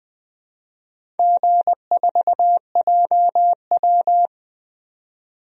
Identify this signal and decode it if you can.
Morse code, heard as a keyed tone, about 20 wpm, 700 Hz. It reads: G4JW
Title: DXCC-100 Call Signs - 4 Characters - 20wpm